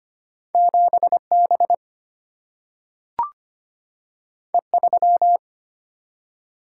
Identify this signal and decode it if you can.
Morse code, heard as a keyed tone, about 25 wpm, 700 Hz. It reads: 7B E3
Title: Single-Letter Single-Number 25wpm